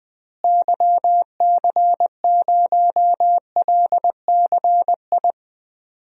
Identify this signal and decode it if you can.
Morse code, heard as a keyed tone, about 20 wpm, 700 Hz. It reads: YC0LCI
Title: DXCC-100 Call Signs - 6 Characters - Repeated 3x - 20wpm